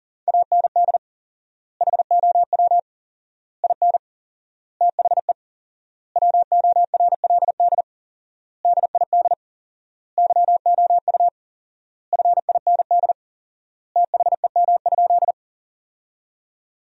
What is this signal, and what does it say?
Morse code, heard as a keyed tone, about 37 wpm, 700 Hz. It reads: AND HOW IN THE WORLD DID YOU FIND THEM?
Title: 4x Word-Spacing - Sentences from Top 200 Words - 40wpm